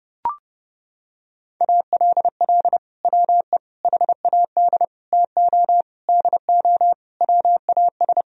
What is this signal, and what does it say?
Morse code, heard as a keyed tone, about 32 wpm, 700 Hz. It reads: ALL WE HAD TO DO WAS
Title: Sentences from Top 1000 Words - Review Part 4 - 30wpm